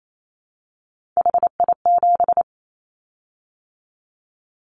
Morse code, heard as a keyed tone, about 28 wpm, 700 Hz. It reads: HI7
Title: Callsign Prefixes for Top 100 DXCC - 28wpm